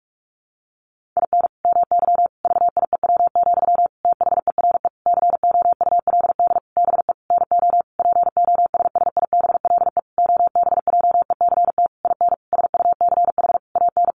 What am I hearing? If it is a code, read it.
Morse code, heard as a keyed tone, about 44 wpm, 700 Hz. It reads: IN MY VIEW, THERE COULD BE NO POSSIBLE OBJECT IN SUCH AN
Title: Sentences from Top 1500 Words 45wpm